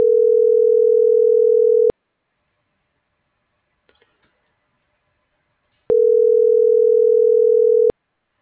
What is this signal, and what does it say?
Ringback tone (North America), 440 + 480 Hz.